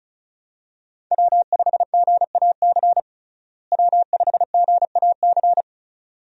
Morse code, heard as a keyed tone, about 36 wpm, 700 Hz. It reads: W5GAC W5GAC
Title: USA Call Signs - 5 Characters - Repeated 2x - 35wpm